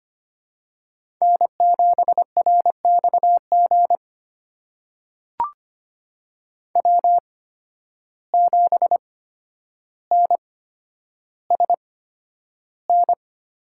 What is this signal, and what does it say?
Morse code, heard as a keyed tone, about 25 wpm, spaced as for 11 wpm, 700 Hz. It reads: N7RXG W7NSN
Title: USA Call Signs - 5 Characters - 8x Character Spacing - 25wpm